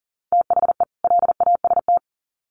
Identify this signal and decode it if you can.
Morse code, heard as a keyed tone, about 40 wpm, 700 Hz. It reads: THE LAST
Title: Sets of 2 Words 40wpm